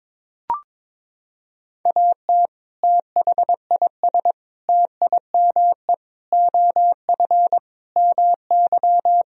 Morse code, heard as a keyed tone, about 22 wpm, 700 Hz. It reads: AT THIS TIME OF MY
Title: Sentences from Top 300 Words and Abbreviations 22wpm